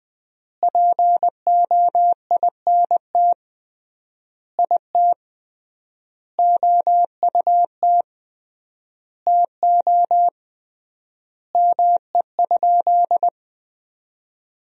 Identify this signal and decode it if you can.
Morse code, heard as a keyed tone, about 20 wpm, 700 Hz. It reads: POINT IT OUT TO ME?
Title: 3x Word-Spacing - Sentences from Top 200 Words - 20wpm